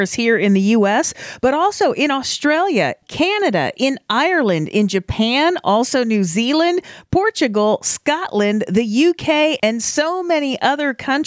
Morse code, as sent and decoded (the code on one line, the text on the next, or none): none